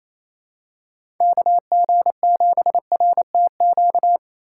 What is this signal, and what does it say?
Morse code, heard as a keyed tone, about 28 wpm, 700 Hz. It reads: KG7RTQ
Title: USA Call Signs - 6 Characters - 3x Character Spacing - 28wpm